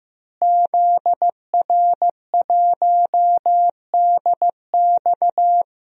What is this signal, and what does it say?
Morse code, heard as a keyed tone, about 15 wpm, 700 Hz. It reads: ZR1DX
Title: DXCC-100 Call Signs - 5 Characters - Repeated 3x - 15wpm